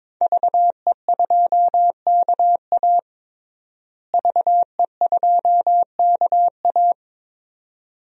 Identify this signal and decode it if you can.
Morse code, heard as a keyed tone, about 22 wpm, 700 Hz. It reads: VE2KA VE2KA
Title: DXCC-100 Call Signs - 5 Characters - Repeated 2x - 22wpm